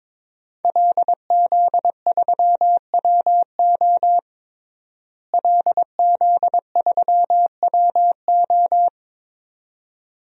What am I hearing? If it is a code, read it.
Morse code, heard as a keyed tone, about 22 wpm, 700 Hz. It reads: LZ3WO LZ3WO
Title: DXCC-100 Call Signs - 5 Characters - Repeated 2x - 22wpm